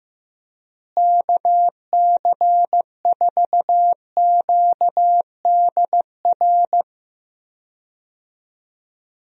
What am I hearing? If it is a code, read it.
Morse code, heard as a keyed tone, about 15 wpm, 700 Hz. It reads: KC4QDR
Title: USA Call Signs - 6 Characters - Repeated 2x - 15wpm